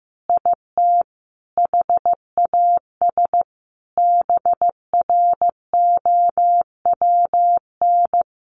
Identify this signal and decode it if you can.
Morse code, heard as a keyed tone, about 15 wpm, 700 Hz. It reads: IT HAS BROWN
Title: Easy Phrases 15wpm